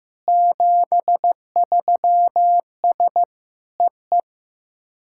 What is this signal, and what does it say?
Morse code, heard as a keyed tone, about 15 wpm, 700 Hz. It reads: 73S EE
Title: SOTA QSO Practice 15wpm